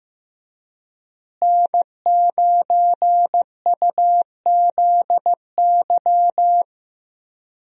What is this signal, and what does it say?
Morse code, heard as a keyed tone, about 15 wpm, 700 Hz. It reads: N9UZY
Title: USA Call Signs - 5 Characters - Repeated 3x - 15wpm